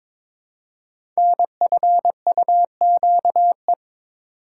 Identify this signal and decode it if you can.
Morse code, heard as a keyed tone, about 22 wpm, 700 Hz. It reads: NFUQE
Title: Code Groups - 5 Letter:Numbers - 22wpm